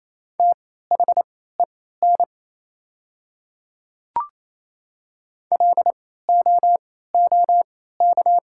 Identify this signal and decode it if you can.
Morse code, heard as a keyed tone, about 28 wpm, spaced as for 16 wpm, 690 Hz. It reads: THEN LOOK
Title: 3x Character Spacing - Top 100 Words - 28wpm